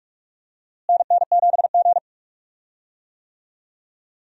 Morse code, heard as a keyed tone, about 46 wpm, 700 Hz. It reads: NN7G
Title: USA Call Signs - 4 Characters - Repeated 2x - 45wpm